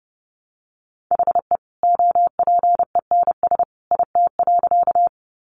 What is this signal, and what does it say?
Morse code, heard as a keyed tone, about 30 wpm, 700 Hz. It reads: HE OPENS IT.
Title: Additional Easy Phrases 30wpm